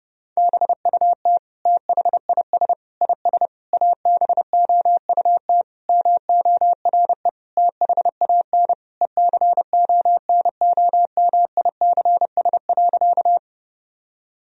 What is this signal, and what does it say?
Morse code, heard as a keyed tone, about 30 wpm, 700 Hz. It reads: BUT THIS IS ABOUT MORE THAN ECONOMICS.
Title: Sentences from Top 1000 Words - Review Part 3 - 30wpm